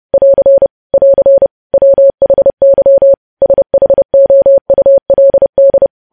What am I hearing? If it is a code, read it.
Morse code, heard as a keyed tone, about 30 wpm, 550 Hz. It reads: <AR> <AR> WHY SHOULD
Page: News Headlines in Morse Code at 30 WPM